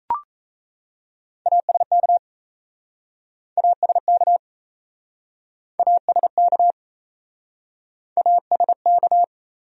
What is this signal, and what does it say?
Morse code, heard as a keyed tone, about 38 wpm, 700 Hz. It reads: ASK ASK ASK ASK